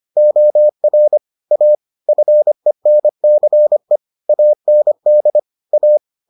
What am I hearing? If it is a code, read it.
Morse code, heard as a keyed tone, about 25 wpm, 600 Hz. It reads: OR A FENCE AND A